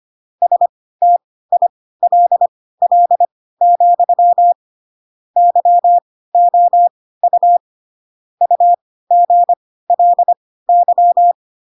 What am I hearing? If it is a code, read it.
Morse code, heard as a keyed tone, about 25 wpm, spaced as for 16 wpm, 700 Hz. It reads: STILL, YOU UGLY
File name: Pinocchio / Pinocchio ch4 10 25 ik1hge